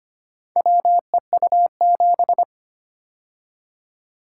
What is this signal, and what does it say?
Morse code, heard as a keyed tone, about 25 wpm, 700 Hz. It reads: WEU7
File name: practice / 25wpm / Four Letter-Number Combinations - 25wpm